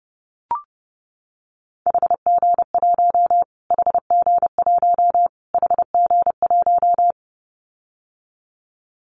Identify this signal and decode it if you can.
Morse code, heard as a keyed tone, about 30 wpm, 700 Hz. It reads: HG1 HG1 HG1
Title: Callsign Prefixes for Top 25 DXCC - Repeated 3x - 30wpm